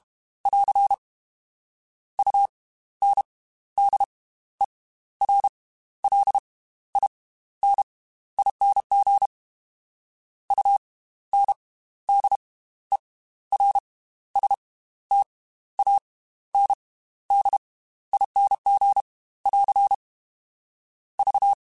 Morse code, heard as a keyed tone, about 32 wpm, spaced as for 14 wpm, 800 Hz. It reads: <AR> UNDERLINING UNDERSTANDING <AR> V